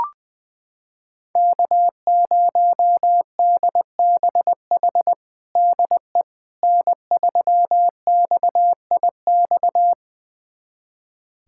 Morse code, heard as a keyed tone, about 21 wpm, 700 Hz. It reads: K0DBH DE N3XIX
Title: USA Callsign DE Callsign - 5 Characters - 20wpm